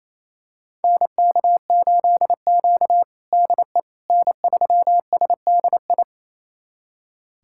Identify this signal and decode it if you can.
Morse code, heard as a keyed tone, about 27 wpm, 700 Hz. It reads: NK8Q DE N3SDI
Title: USA Callsign DE Callsign 28wpm